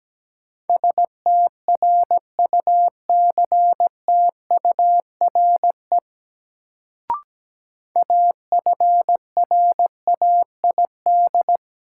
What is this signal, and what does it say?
Morse code, heard as a keyed tone, about 17 wpm, 700 Hz. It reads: STRUCTURE AFRAID
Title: Sentences from Top 1500 Words - New Word Drill Alt - 17wpm